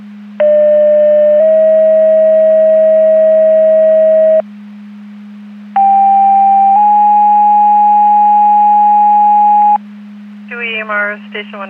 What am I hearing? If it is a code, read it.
Two-tone page, heard as a one-second tone, then a three-second tone, about 600 then 640 Hz, about 790 then 830 Hz, with speech. It is heard from 0.4 to 9.8 s.